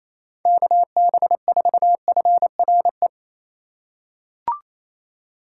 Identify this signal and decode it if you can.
Morse code, heard as a keyed tone, about 28 wpm, 700 Hz. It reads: KB4FRE
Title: USA Call Signs - 6x Character Spacing - 28wpm